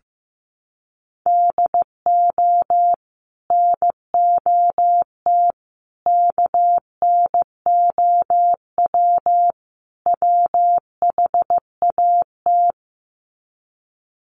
Morse code, heard as a keyed tone, about 15 wpm, 700 Hz. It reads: DO NOT KNOW WHAT
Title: Sets of 4 Words 15wpm